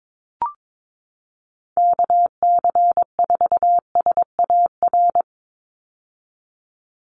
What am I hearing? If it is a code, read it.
Morse code, heard as a keyed tone, about 22 wpm, 700 Hz. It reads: KC4SAR
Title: USA Call Signs 22wpm